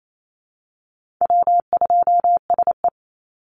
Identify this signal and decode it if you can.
Morse code, heard as a keyed tone, about 28 wpm, 700 Hz. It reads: W2SE
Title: NA SOTA Chaser Callsigns 28wpm